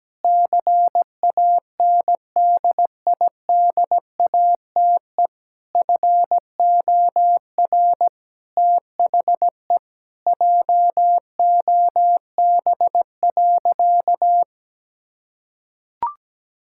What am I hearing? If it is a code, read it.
Morse code, heard as a keyed tone, about 17 wpm, 700 Hz. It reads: CANDIDATE FOR THE JOB.
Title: Sentences from Top 1600 Words 17wpm